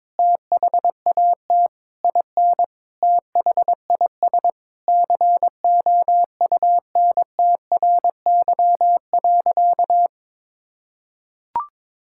Morse code, heard as a keyed tone, about 22 wpm, 700 Hz. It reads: THAT IN THIS COUNTRY.
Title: Sentences from Top 1000 Words - Review Part 5 - 22wpm